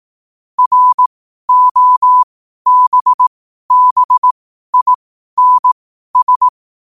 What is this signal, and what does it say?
Morse code, heard as a keyed tone, about 18 wpm, spaced as for 12 wpm, 1000 Hz. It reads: ROBBINS
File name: qotdfiles / 12wpm